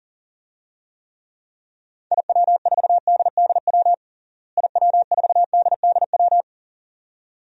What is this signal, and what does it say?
Morse code, heard as a keyed tone, about 40 wpm, 700 Hz. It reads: IW4DDW IW4DDW
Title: DXCC-100 Call Signs - 6 Characters - Repeated 2x - 40wpm